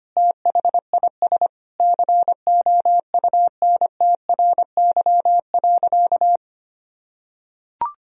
Morse code, heard as a keyed tone, about 25 wpm, 700 Hz. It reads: THIS COUNTRY.
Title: Sentences from Top 1000 Words - Review Part 3 - 25wpm